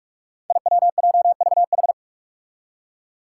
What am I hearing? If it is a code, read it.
Morse code, heard as a keyed tone, about 45 wpm, 700 Hz. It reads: IWJVH
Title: Five Challenging Letter-Number Combinations 45wpm